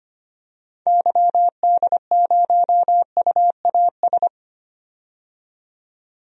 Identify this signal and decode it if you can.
Morse code, heard as a keyed tone, about 25 wpm, 700 Hz. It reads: YD0UAS
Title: DXCC-100 Call Signs - 6 Characters - 25wpm